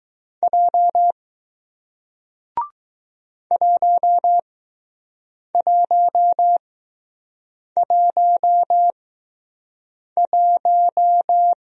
Morse code, heard as a keyed tone, about 20 wpm, spaced as for 6 wpm, 700 Hz. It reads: J 1111